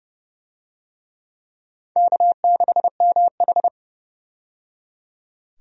Morse code, heard as a keyed tone, about 32 wpm, 700 Hz. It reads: K6MH
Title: USA Call Signs - 4 Characters - 30wpm